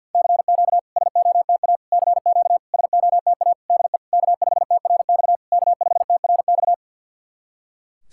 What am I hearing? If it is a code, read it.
Morse code, heard as a keyed tone, about 48 wpm, 690 Hz. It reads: CQ SOTA CQ SOTA DE K5TRX K5TRX
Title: SOTA QSO Practice 50wpm